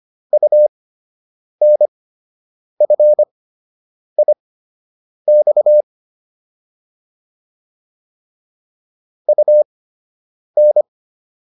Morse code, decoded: UNFIX UN

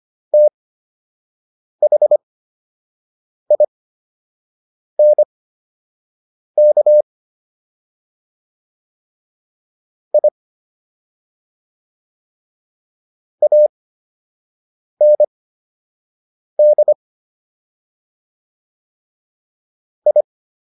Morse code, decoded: THINK I AND I

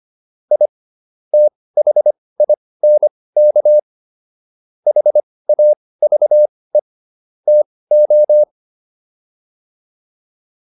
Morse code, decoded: I THINK HAVE TO

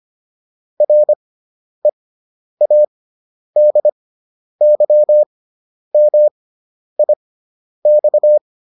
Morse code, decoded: READYMIX

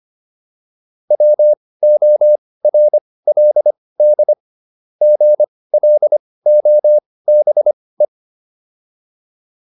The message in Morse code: WORLD GLOBE

